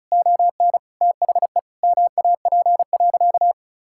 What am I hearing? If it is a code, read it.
Morse code, heard as a keyed tone, about 34 wpm, 700 Hz. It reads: ON THE MAP.
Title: Sentences from Top 1000 Words - Review Part 6 - 35wpm